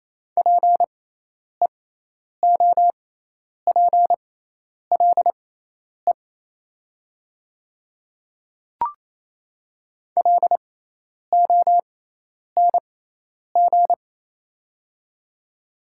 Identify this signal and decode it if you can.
Morse code, heard as a keyed tone, about 28 wpm, spaced as for 10 wpm, 700 Hz. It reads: PEOPLE LONG
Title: 6x Character Spacing - Top 500 Words - 28wpm